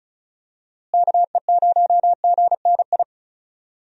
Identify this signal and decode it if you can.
Morse code, heard as a keyed tone, about 36 wpm, 700 Hz. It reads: KE0GNI
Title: USA Call Signs - 6 Characters - Repeated 2x - 35wpm